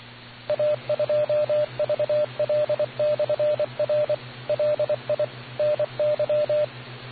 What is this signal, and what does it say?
Morse code, heard as a keyed tone, about 24 wpm, 600 Hz, with radio noise. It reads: A2VL/R LI NY